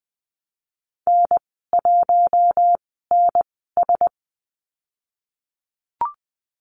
Morse code, heard as a keyed tone, about 20 wpm, spaced as for 15 wpm, 700 Hz. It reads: N1NS
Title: 042 Lesson 8g - Number 1 - Call Signs 20wpm